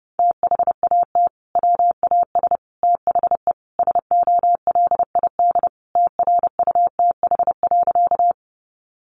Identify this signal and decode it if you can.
Morse code, heard as a keyed tone, about 30 wpm, 700 Hz. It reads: THAT WAS THE SOLID TRUTH.
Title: Sentences from Top 1500 Words 30wpm